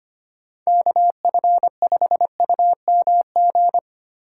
Morse code, heard as a keyed tone, about 25 wpm, 700 Hz. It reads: KF5UMG
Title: USA Call Signs - 6 Characters - 3x Character Spacing - 25wpm